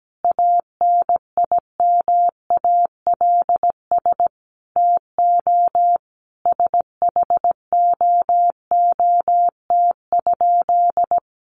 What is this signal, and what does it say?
Morse code, heard as a keyed tone, about 17 wpm, 700 Hz. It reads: ANIMALS TO SHOOT?
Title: A Sound Of Thunder 17wpm